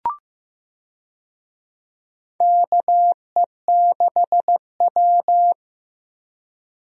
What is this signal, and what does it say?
Morse code, heard as a keyed tone, about 15 wpm, 700 Hz. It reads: KE6W
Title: USA Call Signs - 4 Characters - Repeated 2x - 15wpm